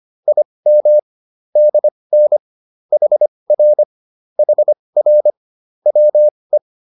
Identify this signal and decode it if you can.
Morse code, heard as a keyed tone, about 25 wpm, spaced as for 20 wpm, 600 Hz. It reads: IM DN HR HR WE
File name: PR102a 20